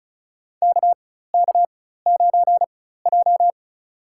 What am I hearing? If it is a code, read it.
Morse code, heard as a keyed tone, about 34 wpm, spaced as for 16 wpm, 700 Hz. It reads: KK9J
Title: USA Call Signs - 4 Characters - 4x Character Spacing - 35wpm